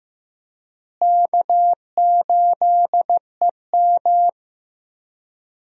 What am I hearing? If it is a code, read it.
Morse code, heard as a keyed tone, about 16 wpm, 700 Hz. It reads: K8EM
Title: USA Call Signs - 4 Characters - Repeated 3x - 15wpm